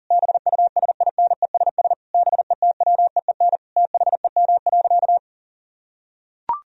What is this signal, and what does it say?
Morse code, heard as a keyed tone, about 40 wpm, 700 Hz. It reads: BUSINESS BETWEEN THEM.
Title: Sentences from Top 1000 Words - Review Part 1 - 40wpm